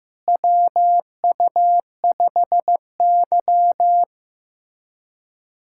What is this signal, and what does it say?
Morse code, heard as a keyed tone, about 15 wpm, 700 Hz. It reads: WU5Y
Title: USA Callsign DE Callsign - 4 Characters - 15wpm